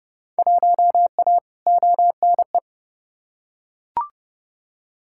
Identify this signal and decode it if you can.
Morse code, heard as a keyed tone, about 32 wpm, 700 Hz. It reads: 1A ONE
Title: Field Day Exchanges - 30wpm